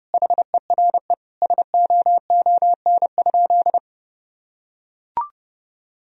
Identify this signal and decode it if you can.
Morse code, heard as a keyed tone, about 30 wpm, 700 Hz. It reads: HERE SOON?
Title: Sentences from Top 1000 Words - Review Part 3 - 30wpm